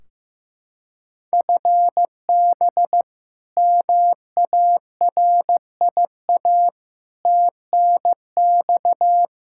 Morse code, heard as a keyed tone, about 15 wpm, 700 Hz. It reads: FB MARIA TNX
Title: K1USN SST QSO Practice 15wpm